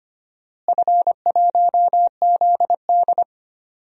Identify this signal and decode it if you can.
Morse code, heard as a keyed tone, about 25 wpm, 700 Hz. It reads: F1ZD
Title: DXCC-100 Call Signs - 4 Characters - Repeated 2x - 25wpm